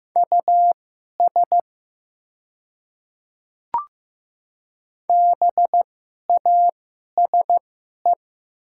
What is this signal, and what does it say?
Morse code, heard as a keyed tone, about 15 wpm, spaced as for 11 wpm, 700 Hz. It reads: US BASE
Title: 2x Character Spacing - Top 300 Words - 15wpm